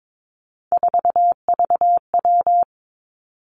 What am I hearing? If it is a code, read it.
Morse code, heard as a keyed tone, about 22 wpm, 700 Hz. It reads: 4VW
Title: Three Challenging Letter-Number Combinations 22wpm